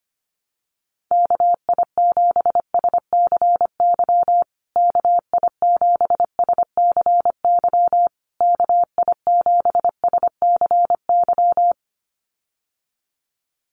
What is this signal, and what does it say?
Morse code, heard as a keyed tone, about 25 wpm, 700 Hz. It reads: KI7SCY KI7SCY KI7SCY
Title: USA Call Signs Repeated 3x 25wpm